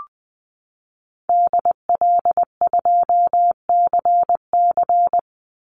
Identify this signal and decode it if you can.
Morse code, heard as a keyed tone, about 19 wpm, 700 Hz. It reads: DL2CC
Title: Call Signs - 20wpm